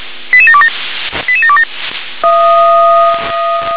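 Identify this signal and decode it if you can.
Five-tone selective call (ZVEI), heard as quick runs of five tones, tones of about 70 ms. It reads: 88727 88727
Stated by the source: ZVEI five-tone call received with an SDR.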